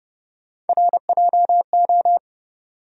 Morse code, heard as a keyed tone, about 30 wpm, 700 Hz. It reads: RJO
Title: USA Call Sign Postfixes - 3 Letters 30wpm